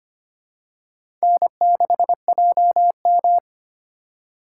Morse code, heard as a keyed tone, about 25 wpm, 700 Hz. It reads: N6JM
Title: USA Call Signs - 2x Character Spacing - 25wpm